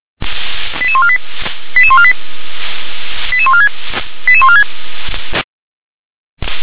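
Five-tone selective call (ZVEI), heard as quick runs of five tones, tones of about 70 ms. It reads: 88148 88148 88146 88146
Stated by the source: SDR-received ZVEI five-tone selective call, code 88783.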